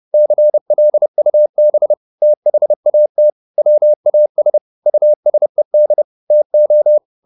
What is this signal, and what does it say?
Morse code, heard as a keyed tone, about 30 wpm, 600 Hz. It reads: CLUB THAT WAS USED TO
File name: the old man and the sea / the old man and the sea 1 30wpm